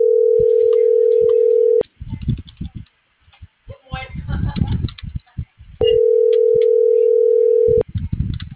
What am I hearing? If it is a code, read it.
Ringback tone (North America), 440 + 480 Hz.